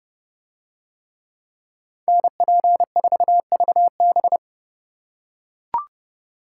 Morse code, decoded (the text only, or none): NP4VB